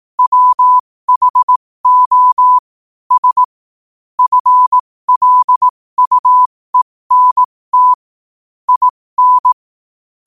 Morse code, heard as a keyed tone, about 18 wpm, spaced as for 15 wpm, 1000 Hz. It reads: WHO S FLUENT IN
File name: qotdfiles / 15wpm